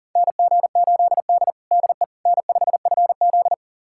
Morse code, heard as a keyed tone, about 40 wpm, 700 Hz. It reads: NG8D DE N5FZ
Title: USA Callsign DE Callsign - 4 Characters - 40wpm